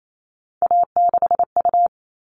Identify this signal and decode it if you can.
Morse code, heard as a keyed tone, about 28 wpm, 700 Hz. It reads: A6U
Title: All Cut Numbers - 3-Digit 28wpm